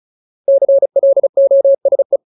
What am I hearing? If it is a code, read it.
Morse code, heard as a keyed tone, about 34 wpm, 550 Hz. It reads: CLOSE
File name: Ultimate Word List / 35WPM Ultimate